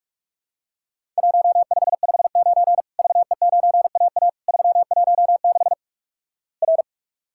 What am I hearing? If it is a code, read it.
Morse code, heard as a keyed tone, about 46 wpm, 700 Hz. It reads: 1559 VE9AA 316 R
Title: Worked All Europe DX Contest - QTC Exchange - 45wpm